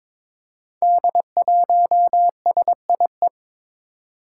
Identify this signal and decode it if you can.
Morse code, heard as a keyed tone, about 22 wpm, 700 Hz. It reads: D1SIE
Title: Five Challenging Letter-Number Combinations 22wpm